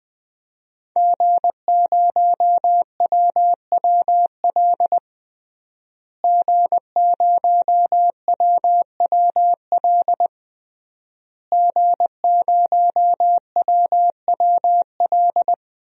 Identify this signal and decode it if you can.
Morse code, heard as a keyed tone, about 20 wpm, 700 Hz. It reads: G0WWL G0WWL G0WWL
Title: DXCC-100 Call Signs - 5 Characters - Repeated 3x - 20wpm